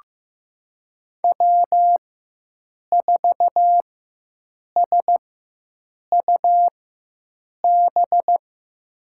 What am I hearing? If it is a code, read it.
Morse code, heard as a keyed tone, about 15 wpm, spaced as for 7 wpm, 700 Hz. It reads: W4SUB
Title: USA Call Signs - 4x Character Spacing - 15wpm